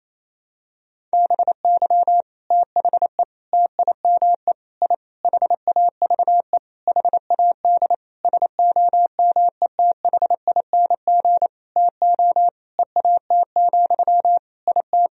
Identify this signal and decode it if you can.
Morse code, heard as a keyed tone, about 28 wpm, 700 Hz. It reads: BY THE TIME I HAVE HAD SOMETHING TO EAT, IT
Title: Sentences from Top 1000 Words - Review Part 1 - 28wpm